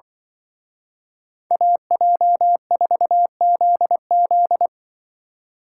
Morse code, heard as a keyed tone, about 24 wpm, 700 Hz. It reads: AJ4ZZ
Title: Speed Racing - USA 2x2 Call Signs 20wpm